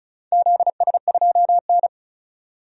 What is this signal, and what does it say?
Morse code, heard as a keyed tone, about 36 wpm, 700 Hz. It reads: ZS2N
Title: DXCC-100 Call Signs - 4 Characters - 35wpm